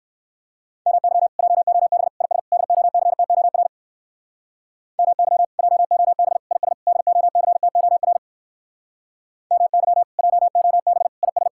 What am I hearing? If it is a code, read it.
Morse code, heard as a keyed tone, about 59 wpm, 700 Hz. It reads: MY JOB IS DOCTOR MY JOB IS DOCTOR MY JOB IS